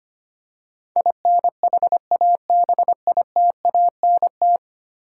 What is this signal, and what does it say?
Morse code, heard as a keyed tone, about 25 wpm, 700 Hz. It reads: INHABITANT
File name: 10 Letter Words - First 3 Letters Given - 25wpm